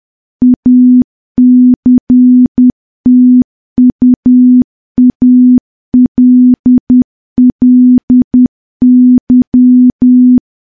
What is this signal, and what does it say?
Morse code, heard as a keyed tone, about 10 wpm, 250 Hz. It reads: ACTUALLY
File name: morse / morse-training-10WPM